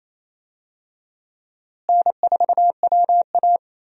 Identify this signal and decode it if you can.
Morse code, heard as a keyed tone, about 28 wpm, 700 Hz. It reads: N4WA
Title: USA Call Signs - 4 Characters - Repeated 3x - 28wpm